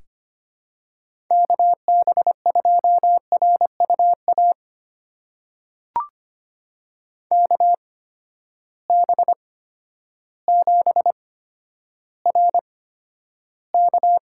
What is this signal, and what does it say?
Morse code, heard as a keyed tone, about 25 wpm, spaced as for 11 wpm, 700 Hz. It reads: KB2RUA KB7RK
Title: USA Call Signs - 6 Characters - 8x Character Spacing - 25wpm